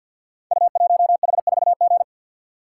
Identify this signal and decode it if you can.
Morse code, heard as a keyed tone, about 52 wpm, 690 Hz. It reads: U1H4G
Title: Code Groups - 5 Letter:Numbers - 50wpm